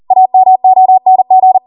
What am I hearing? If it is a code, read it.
Morse code, heard as a keyed tone, about 40 wpm, 750 Hz. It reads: AMONG